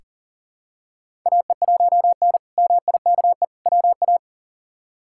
Morse code, heard as a keyed tone, about 40 wpm, 700 Hz. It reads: AE1N MIKE WA
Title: K1USN SST QSO Practice - 40wpm